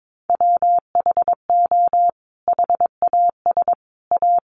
Morse code, heard as a keyed tone, about 22 wpm, 700 Hz. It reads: WHO HAS A
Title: Sentences from Top 1100 Words 22wpm